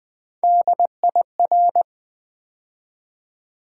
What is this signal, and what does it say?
Morse code, heard as a keyed tone, about 20 wpm, 700 Hz. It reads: DIR
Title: Build A Word - 3 Letters At A Time - 8-Letter Words - 20wpm